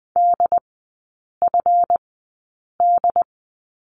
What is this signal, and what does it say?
Morse code, heard as a keyed tone, about 20 wpm, spaced as for 8 wpm, 700 Hz. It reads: DFD
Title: 159 Lesson 26e - Letter g - Three Characters 20wpm